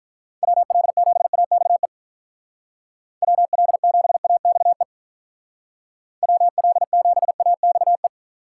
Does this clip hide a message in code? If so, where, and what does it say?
Yes: Morse code from 0.4 to 8.1 s, heard as a keyed tone, about 45 wpm, 700 Hz. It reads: WL7AXE WL7AXE WL7AXE